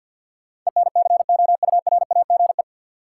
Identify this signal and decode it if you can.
Morse code, heard as a keyed tone, about 48 wpm, 700 Hz. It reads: ENCOURAGE
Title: Sentences from Top 1400 Words - New Word Drill Alt - 50wpm